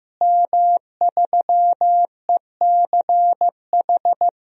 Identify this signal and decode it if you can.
Morse code, heard as a keyed tone, about 15 wpm, 700 Hz. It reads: M3ECH
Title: DXCC-100 Call Signs - 5 Characters - Repeated 2x - 15wpm